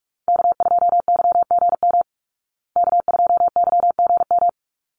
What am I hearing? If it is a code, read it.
Morse code, heard as a keyed tone, about 44 wpm, 700 Hz. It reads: K2YGM K2YGM
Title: Call Signs Repeated 3x - 45wpm